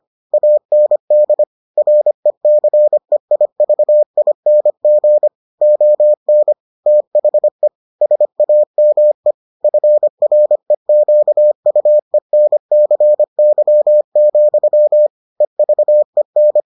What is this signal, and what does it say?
Morse code, heard as a keyed tone, about 25 wpm, 600 Hz. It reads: AND RECEIVING ON THE SAME FREQUENCY, EVEN